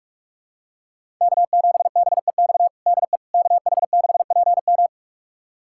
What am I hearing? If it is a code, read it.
Morse code, heard as a keyed tone, about 46 wpm, 700 Hz. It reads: K7BEX DE KH6PM
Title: USA Callsign DE Callsign - 5 Characters - 45wpm